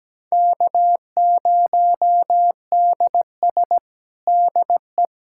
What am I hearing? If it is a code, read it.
Morse code, heard as a keyed tone, about 17 wpm, 700 Hz. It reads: K0DS DE
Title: USA Callsign DE Callsign - 4 Characters - 17wpm